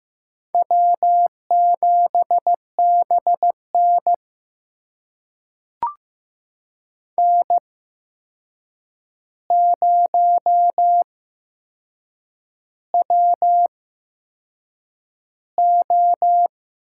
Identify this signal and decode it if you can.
Morse code, heard as a keyed tone, about 15 wpm, spaced as for 6 wpm, 700 Hz. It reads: W7BN N0WO
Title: USA Call Signs - 4 Characters - 8x Character Spacing - 15wpm